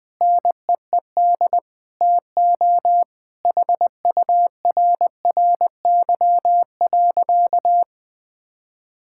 Morse code, heard as a keyed tone, about 20 wpm, 700 Hz. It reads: NEED TO HURRY.
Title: Sentences with 5-Word Phrases - 20wpm